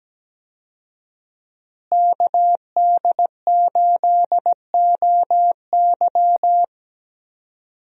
Morse code, heard as a keyed tone, about 17 wpm, 700 Hz. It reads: KD8OY
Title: USA Call Signs - 5 Characters - Repeated 3x - 17wpm